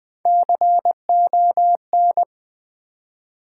Morse code, heard as a keyed tone, about 20 wpm, 700 Hz. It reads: CON
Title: Build A Word - 3 Letters At A Time - 9-Letter Words - 20wpm